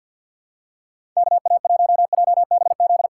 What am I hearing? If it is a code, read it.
Morse code, heard as a keyed tone, about 50 wpm, 700 Hz. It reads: KA1JBZ
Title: USA Call Signs - 2x Character Spacing - 50wpm